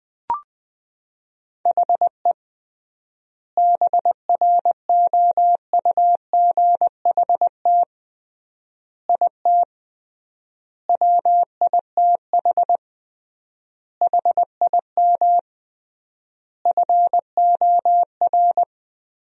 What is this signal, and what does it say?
Morse code, heard as a keyed tone, about 21 wpm, 700 Hz. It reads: HE BROUGHT IT WITH HIM FOR
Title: 3x Word-Spacing - Sentences from Top 500 Words - 20wpm